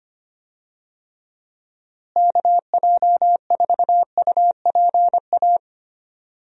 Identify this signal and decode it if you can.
Morse code, heard as a keyed tone, about 25 wpm, 700 Hz. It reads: KJ4UPA
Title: USA Call Signs - 6 Characters - Repeated 2x - 25wpm